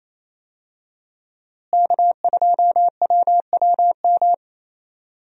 Morse code, heard as a keyed tone, about 27 wpm, 700 Hz. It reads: K2WWM
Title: USA Call Signs - 5 Characters - Repeated 2x - 28wpm